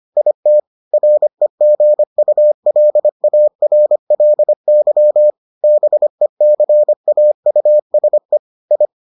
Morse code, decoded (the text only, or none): IT REGULARLY BECAUSE I